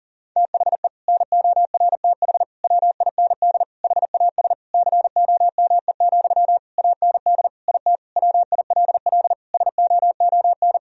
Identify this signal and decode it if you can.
Morse code, heard as a keyed tone, about 40 wpm, 700 Hz. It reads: THE NORTH WIND HAS COME, AND IT WILL SOON
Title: Sentences from Top 1000 Words - Review Part 3 - 40wpm